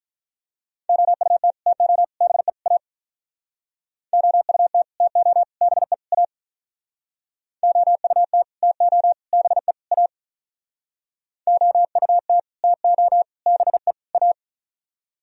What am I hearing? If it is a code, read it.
Morse code, heard as a keyed tone, about 46 wpm, 700 Hz. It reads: OUT TO BE A OUT TO BE A OUT TO BE A OUT TO BE A